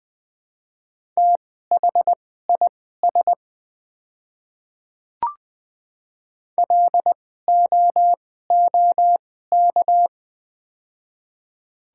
Morse code, heard as a keyed tone, about 20 wpm, spaced as for 14 wpm, 700 Hz. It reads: THIS LOOK
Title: 2x Character Spacing - Top 100 Words - 20wpm